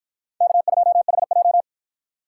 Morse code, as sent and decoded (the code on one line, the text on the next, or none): -.- ..--- .... .---
K2HJ